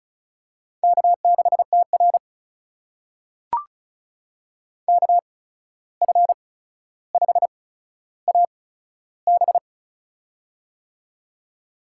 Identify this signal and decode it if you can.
Morse code, heard as a keyed tone, about 36 wpm, spaced as for 14 wpm, 700 Hz. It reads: K6TR KF5AB
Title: USA Call Signs - 8x Character Spacing - 35wpm